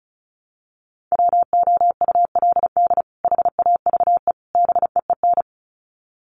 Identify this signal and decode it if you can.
Morse code, heard as a keyed tone, about 34 wpm, 700 Hz. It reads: WOULD HAVE BEEN
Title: Sets of 3 Words 35wpm